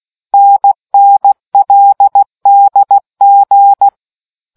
Morse code, heard as a keyed tone, about 16 wpm, 800 Hz. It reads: NNLDG